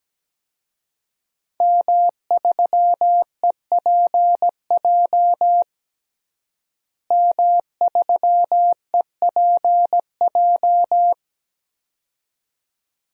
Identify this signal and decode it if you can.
Morse code, heard as a keyed tone, about 17 wpm, 700 Hz. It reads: M3EPJ M3EPJ
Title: DXCC-100 Call Signs - 5 Characters - Repeated 2x - 17wpm